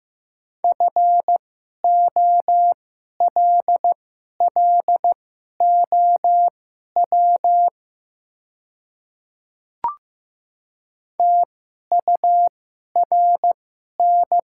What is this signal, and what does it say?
Morse code, heard as a keyed tone, about 15 wpm, spaced as for 11 wpm, 700 Hz. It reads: FOLLOW TURN
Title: 2x Character Spacing - Top 200 Words - 15wpm